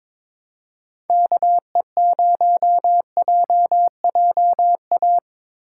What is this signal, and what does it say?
Morse code, heard as a keyed tone, about 22 wpm, 700 Hz. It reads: KE0JJA
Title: USA Call Signs - 3x Character Spacing - 22wpm